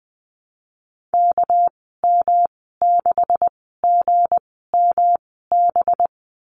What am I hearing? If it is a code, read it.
Morse code, heard as a keyed tone, about 19 wpm, spaced as for 14 wpm, 700 Hz. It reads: KM6GMB
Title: 219 Lesson 35g - Number 0 - Call Signs 20wpm